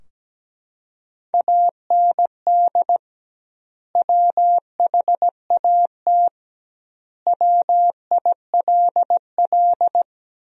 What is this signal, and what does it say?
Morse code, heard as a keyed tone, about 17 wpm, 700 Hz. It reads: AND WHAT WILL
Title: 2x Word-Spacing - Sentences from Top 200 Words - 17wpm